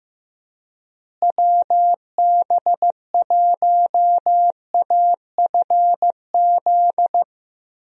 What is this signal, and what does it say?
Morse code, heard as a keyed tone, about 15 wpm, 700 Hz. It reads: WB1AFZ
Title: USA Call Signs - 6 Characters - Repeated 3x - 15wpm